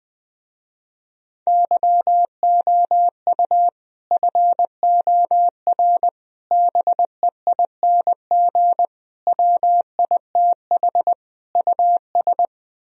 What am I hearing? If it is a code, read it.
Morse code, heard as a keyed tone, about 20 wpm, 700 Hz. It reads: YOU FOR BEING WITH US